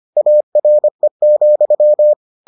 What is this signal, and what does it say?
Morse code, heard as a keyed tone, about 25 wpm, 600 Hz. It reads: ARE,